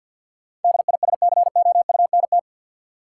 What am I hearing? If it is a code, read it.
Morse code, heard as a keyed tone, about 53 wpm, 700 Hz. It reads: DISCOUNT
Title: Sentences from Top 1200 Words - New Word Drill - 50wpm